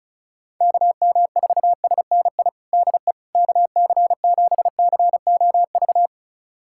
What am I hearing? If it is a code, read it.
Morse code, heard as a keyed tone, about 36 wpm, 700 Hz. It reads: KM4SNI DE KC7COV
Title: USA Callsign DE Callsign 35wpm